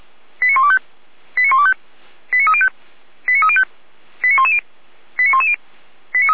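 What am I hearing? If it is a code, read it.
Five-tone selective call (ZVEI), heard as quick runs of five tones, tones of about 80 ms. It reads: 89125 89125 89305 89305 89110 89110
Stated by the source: ZVEI five-tone call 89276 received via SDR, Feuerwehr Orlamünde fire brigade.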